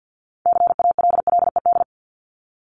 Morse code, heard as a keyed tone, about 50 wpm, 700 Hz. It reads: CALLED
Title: ED Suffix - Top 100 Words with Base Words 50wpm